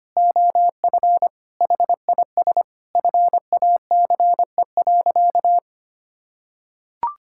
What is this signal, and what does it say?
Morse code, heard as a keyed tone, about 25 wpm, 700 Hz. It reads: OF HIS FACE.
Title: Sentences from Top 1000 Words - Review Part 5 - 25wpm